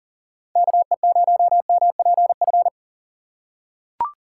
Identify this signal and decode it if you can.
Morse code, heard as a keyed tone, about 40 wpm, 700 Hz. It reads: KE0MPF
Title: USA Call Signs - 6 Characters - 3x Character Spacing - 40wpm